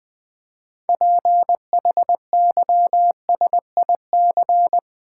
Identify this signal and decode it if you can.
Morse code, heard as a keyed tone, about 20 wpm, 700 Hz. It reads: PHYSIC
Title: Build A Word - 2 Letters At A Time - 8-Letter Words - 20wpm